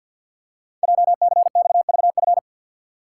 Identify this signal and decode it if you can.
Morse code, heard as a keyed tone, about 48 wpm, 700 Hz. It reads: JCXVF
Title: Code Groups - 5 Letters - 50wpm